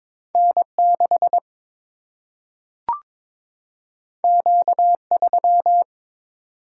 Morse code, heard as a keyed tone, about 22 wpm, 700 Hz. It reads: N6 Q3
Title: Single-Letter Single-Number 22wpm